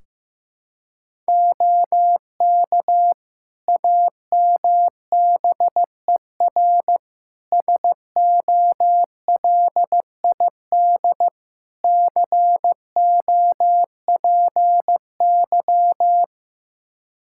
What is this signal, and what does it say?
Morse code, heard as a keyed tone, about 15 wpm, 700 Hz. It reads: OK AMBER SOLID COPY
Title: Practice QSO - Exchange 3 - More Info - 15wpm